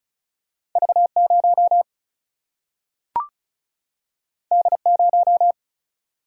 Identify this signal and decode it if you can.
Morse code, heard as a keyed tone, about 36 wpm, 700 Hz. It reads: V0 D0
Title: Single-Letter Single-Number 35wpm